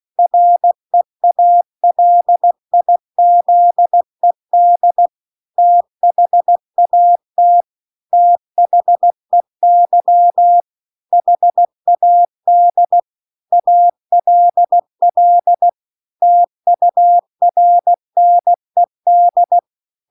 Morse code, heard as a keyed tone, about 16 wpm, 700 Hz. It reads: REALIZED THAT THEY HAD ALL TURNED